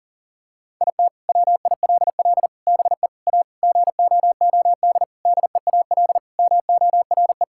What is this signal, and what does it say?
Morse code, heard as a keyed tone, about 37 wpm, 700 Hz. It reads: IT WILL BE A GOOD DEAL MORE
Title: Sentences from Top 1000 Words - Review Part 6 - 40wpm